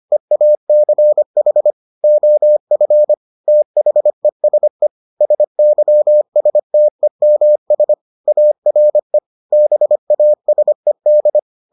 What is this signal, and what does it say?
Morse code, heard as a keyed tone, about 25 wpm, 600 Hz. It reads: EACH OF THESE SYSTEMS ARE BASED